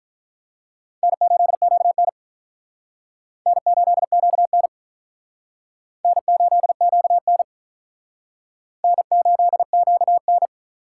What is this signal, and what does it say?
Morse code, heard as a keyed tone, about 46 wpm, 700 Hz. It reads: N8QN N8QN N8QN N8QN